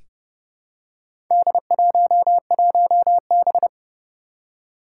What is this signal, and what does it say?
Morse code, heard as a keyed tone, about 30 wpm, 690 Hz. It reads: D11B
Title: Four Challenging Letter-Number Combinations 30wpm